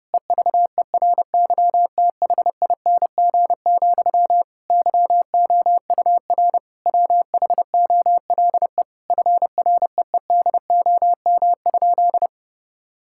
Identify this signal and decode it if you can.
Morse code, heard as a keyed tone, about 30 wpm, 700 Hz. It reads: EVERYTHING, YOUR WHOLE FREEDOM?
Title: Sentences from Top 1000 Words - Review Part 5 - 30wpm